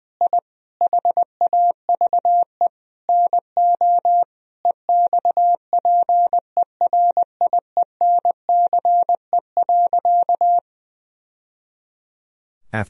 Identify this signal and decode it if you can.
Morse code, heard as a keyed tone, about 20 wpm, 700 Hz. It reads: I HAVE NO EXPERIENCE.
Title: Sentences from Top 1000 Words - Review Part 3 - 20wpm